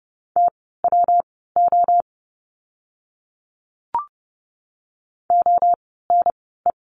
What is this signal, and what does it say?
Morse code, heard as a keyed tone, about 30 wpm, spaced as for 17 wpm, 700 Hz. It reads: TWO ONE
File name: course / 30:10wpm / 115 Lesson 19f - Letter w - Words 30wpm